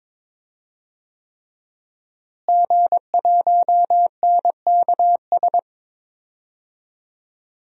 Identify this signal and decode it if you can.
Morse code, heard as a keyed tone, about 22 wpm, 700 Hz. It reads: G1NKS
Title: DXCC-100 Call Signs - 5 Characters - Repeated 3x - 22wpm